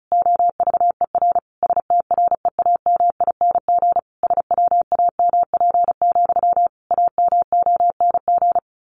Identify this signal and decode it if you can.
Morse code, heard as a keyed tone, about 34 wpm, 700 Hz. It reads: OVER STREAMING SWAMP, AMONG
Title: A Sound Of Thunder 35wpm